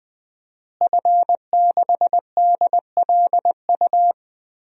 Morse code, heard as a keyed tone, about 20 wpm, 700 Hz. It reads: F6DLU
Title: DXCC-100 Call Signs - 5 Characters - Repeated 2x - 20wpm